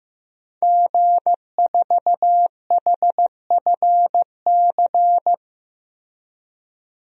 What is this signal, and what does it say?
Morse code, heard as a keyed tone, about 15 wpm, 700 Hz. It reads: G4HFC
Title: DXCC-100 Call Signs - 5 Characters - Repeated 3x - 15wpm